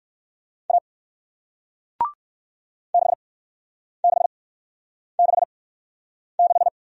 Morse code, heard as a keyed tone, about 58 wpm, spaced as for 9 wpm, 700 Hz. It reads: A 6666